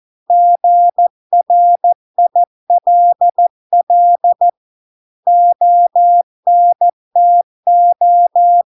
Morse code, heard as a keyed tone, about 14 wpm, 700 Hz. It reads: GRILL ONTO